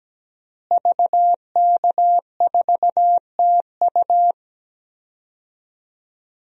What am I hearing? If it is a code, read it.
Morse code, heard as a keyed tone, about 17 wpm, 700 Hz. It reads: VK4TU
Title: DXCC-100 Call Signs - 5 Characters - Repeated 3x - 17wpm